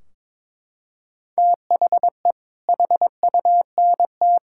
Morse code, heard as a keyed tone, about 21 wpm, 700 Hz. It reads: THE HUNT
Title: Sentences from Top 1600 Words 22wpm